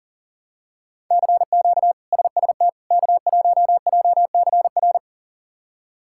Morse code, heard as a keyed tone, about 40 wpm, 700 Hz. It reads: CQ SST K1JCR
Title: K1USN SST QSO Practice - 40wpm